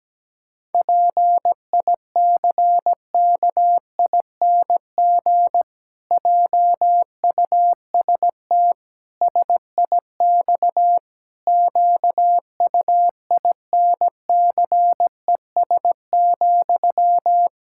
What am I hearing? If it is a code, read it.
Morse code, heard as a keyed tone, about 17 wpm, 700 Hz. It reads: PICKING JUST SIX QUINCES,
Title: Pangram Challenge v2 - 17wpm